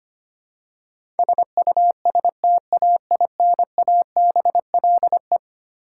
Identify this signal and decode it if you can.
Morse code, heard as a keyed tone, about 25 wpm, 700 Hz. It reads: SUSTAINABLE
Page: morse-code-ninja-practice-25wpm